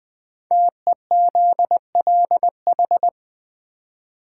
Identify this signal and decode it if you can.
Morse code, heard as a keyed tone, about 20 wpm, 700 Hz. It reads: TEZLH
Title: Code Groups - 5 Letters - 20wpm